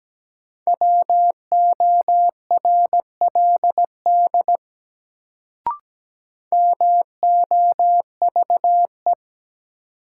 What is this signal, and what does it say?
Morse code, heard as a keyed tone, about 17 wpm, 700 Hz. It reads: WORLD MOVE
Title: Sentences from Top 200 Words - New Word Drill - 17wpm